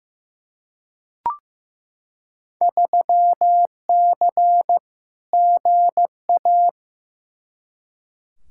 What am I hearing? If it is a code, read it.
Morse code, heard as a keyed tone, about 16 wpm, 700 Hz. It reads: 3C GA
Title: Field Day Exchanges Encore - 15wpm